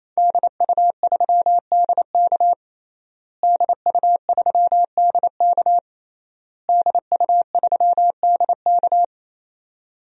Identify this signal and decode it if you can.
Morse code, heard as a keyed tone, about 28 wpm, 690 Hz. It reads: DU3DK DU3DK DU3DK
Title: DXCC-100 Call Signs - 5 Characters - Repeated 3x - 28wpm